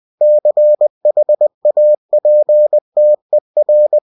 Morse code, heard as a keyed tone, about 20 wpm, 600 Hz. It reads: CHAPTER
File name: cw-books / winnie-the-pooh / winnie-20wpm-page-0111